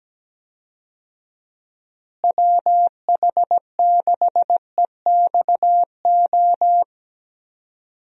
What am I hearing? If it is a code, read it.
Morse code, heard as a keyed tone, about 17 wpm, 700 Hz. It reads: WH6EXO
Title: USA Call Signs - 6 Characters - Repeated 3x - 17wpm